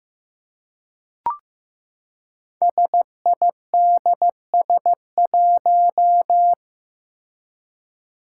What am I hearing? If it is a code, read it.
Morse code, heard as a keyed tone, about 15 wpm, 700 Hz. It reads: SIDS1
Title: Five Challenging Letter-Number Combinations 15wpm